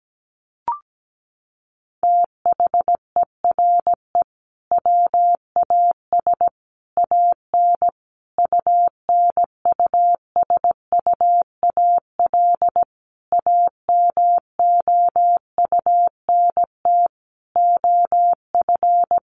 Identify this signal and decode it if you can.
Morse code, heard as a keyed tone, about 17 wpm, 700 Hz. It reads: THERE WAS AN UNUSUAL AMOUNT OF
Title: Sentences from Top 1400 Words 17wpm